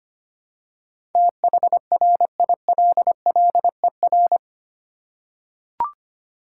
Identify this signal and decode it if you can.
Morse code, heard as a keyed tone, about 25 wpm, 700 Hz. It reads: THRILLER
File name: practice / 25wpm / Difficult Words with Hints - 25wpm